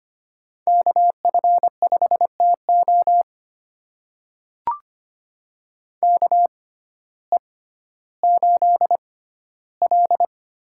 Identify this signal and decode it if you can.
Morse code, heard as a keyed tone, about 25 wpm, spaced as for 14 wpm, 700 Hz. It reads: KF5TO KE8L
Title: USA Call Signs - 5 Characters - 6x Character Spacing - 25wpm